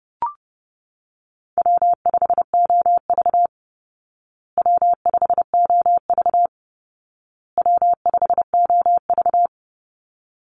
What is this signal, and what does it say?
Morse code, heard as a keyed tone, about 30 wpm, 700 Hz. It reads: W5OV W5OV W5OV
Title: Easy Call Signs Repeated 3x - 30wpm